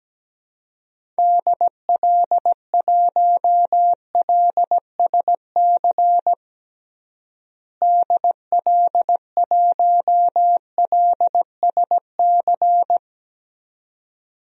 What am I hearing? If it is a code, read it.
Morse code, heard as a keyed tone, about 17 wpm, 700 Hz. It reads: DL1LSC DL1LSC
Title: DXCC-100 Call Signs - 6 Characters - Repeated 2x - 17wpm